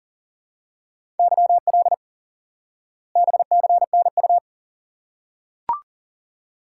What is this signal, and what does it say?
Morse code, heard as a keyed tone, about 37 wpm, 700 Hz. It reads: YL BCNU
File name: IQE Course 3 - Lesson 89 - YAGI - Two Elements - 40wpm